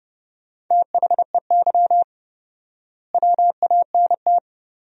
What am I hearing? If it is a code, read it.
Morse code, heard as a keyed tone, about 32 wpm, 700 Hz. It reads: THEY WANT
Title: 4x Word-Spacing - Sentences from Top 200 Words - 30wpm